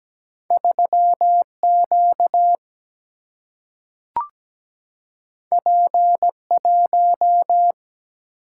Morse code, heard as a keyed tone, about 17 wpm, 700 Hz. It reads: 3Q P1
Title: Single-Letter Single-Number 17wpm